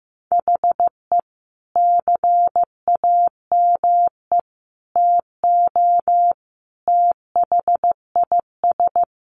Morse code, heard as a keyed tone, about 15 wpm, 700 Hz. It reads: HE CAME TO THIS
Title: A Sound Of Thunder 15wpm